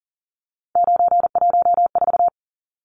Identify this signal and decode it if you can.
Morse code, heard as a keyed tone, about 40 wpm, 700 Hz. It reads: 914
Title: Numbers - 3-Digit 40wpm